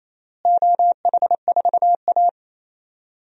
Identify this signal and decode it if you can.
Morse code, heard as a keyed tone, about 28 wpm, 700 Hz. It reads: OH4A
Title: DXCC-100 Call Signs - 4 Characters - Repeated 3x - 28wpm